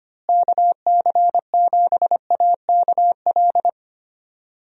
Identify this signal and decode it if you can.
Morse code, heard as a keyed tone, about 25 wpm, 700 Hz. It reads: KC7AKL
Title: USA Call Signs - 6 Characters - Repeated 2x - 25wpm